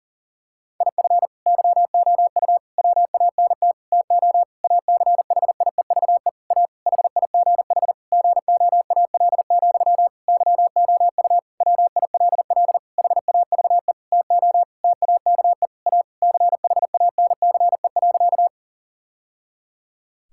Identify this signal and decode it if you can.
Morse code, heard as a keyed tone, about 40 wpm, 690 Hz. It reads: IF YOU WANT TO ACHIEVE A HIGH GOAL, YOU WILL HAVE TO TAKE A CHANCE.
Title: Sentences from Top 1000 Words - Review Part 6 - 40wpm